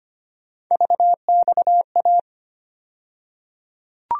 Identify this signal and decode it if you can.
Morse code, heard as a keyed tone, about 25 wpm, 700 Hz. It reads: VXA
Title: USA Call Sign Postfixes - 3 Letters 25wpm